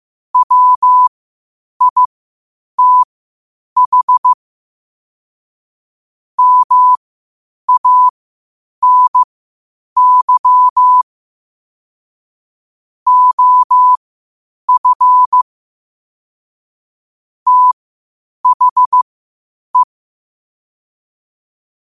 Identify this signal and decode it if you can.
Morse code, heard as a keyed tone, about 15 wpm, spaced as for 8 wpm, 1000 Hz. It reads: WITH MANY OF THE
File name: qotdfiles / 8wpm